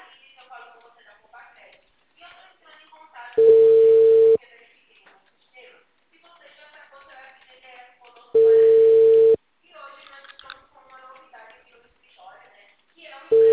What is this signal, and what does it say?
Ringback tone (China, 450 Hz), with speech.
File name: out- -2015- - - .104